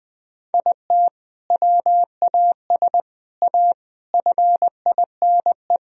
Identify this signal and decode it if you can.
Morse code, heard as a keyed tone, about 20 wpm, 700 Hz. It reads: IT WAS A FINE
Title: Sentences from Top 1000 Words - Review Part 3 - 20wpm